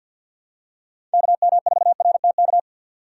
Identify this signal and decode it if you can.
Morse code, heard as a keyed tone, about 52 wpm, 700 Hz. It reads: KM4RTK
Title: USA Call Signs - 3x Character Spacing - 50wpm